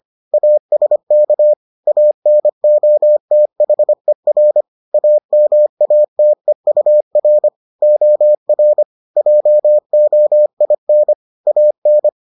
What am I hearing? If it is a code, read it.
Morse code, heard as a keyed tone, about 25 wpm, 600 Hz. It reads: ASK ANOTHER AMATEUR OR JOIN AN